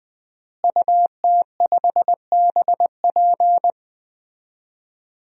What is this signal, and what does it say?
Morse code, heard as a keyed tone, about 20 wpm, 700 Hz. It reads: UT5BP
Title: DXCC-100 Call Signs - 5 Characters - Repeated 3x - 20wpm